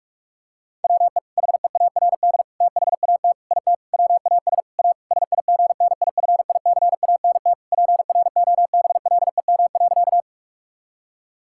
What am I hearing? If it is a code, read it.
Morse code, heard as a keyed tone, about 41 wpm, 700 Hz. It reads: WE HEARD THAT IT WAS A SIGNIFICANT PROBLEM.
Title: Sentences from Top 1000 Words - Review Part 6 - 45wpm